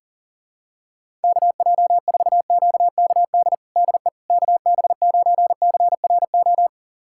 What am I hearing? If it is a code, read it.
Morse code, heard as a keyed tone, about 40 wpm, 700 Hz. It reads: KJ4QKD DE KB9CRO
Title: USA Callsign DE Callsign - 6 Characters - 40wpm